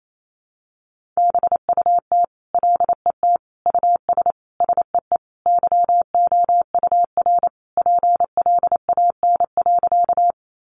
Morse code, heard as a keyed tone, about 28 wpm, 700 Hz. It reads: BUT LET US SEE YOUR PLAN.
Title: Sentences from Top 500 Words 28wpm